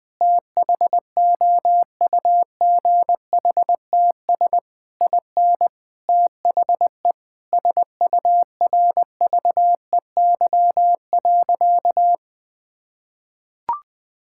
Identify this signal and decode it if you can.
Morse code, heard as a keyed tone, about 20 wpm, 700 Hz. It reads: THOUGHTS IN THE SURVEY.
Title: Sentences with 7-Word Phrases - Spoken Beginning - 20wpm